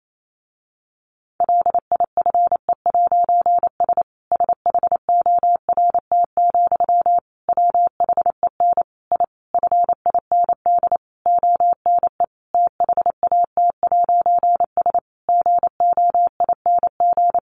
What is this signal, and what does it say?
Morse code, heard as a keyed tone, about 28 wpm, 700 Hz. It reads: LIFE'S SHORT, WHEN I FIND ONE THAT'S GOING
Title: A Sound Of Thunder 28wpm